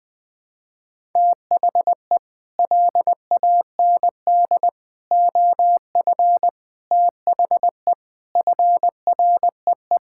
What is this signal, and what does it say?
Morse code, heard as a keyed tone, about 20 wpm, 700 Hz. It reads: THE LAND OF THE FREE
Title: Sets of 9 Words - 20wpm